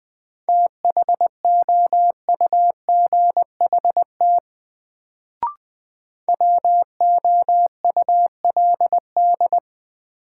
Morse code, heard as a keyed tone, about 20 wpm, 700 Hz. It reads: THOUGHT WOULD
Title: Sets of 2 Words - New Word Drill - 20wpm